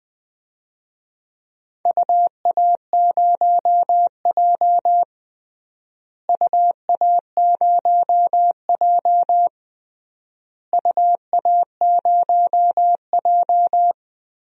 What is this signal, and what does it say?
Morse code, heard as a keyed tone, about 20 wpm, 700 Hz. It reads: UA0J UA0J UA0J
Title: DXCC-100 Call Signs - 4 Characters - Repeated 3x - 20wpm